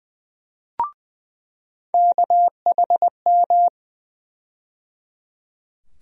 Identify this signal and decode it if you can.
Morse code, heard as a keyed tone, about 20 wpm, 700 Hz. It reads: KHM
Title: USA Call Sign Postfixes - 3 Letters 20wpm